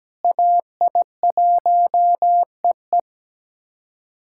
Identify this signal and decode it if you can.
Morse code, heard as a keyed tone, about 17 wpm, 700 Hz. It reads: AI1EE
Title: USA Callsign DE Callsign - 5 Characters - 17wpm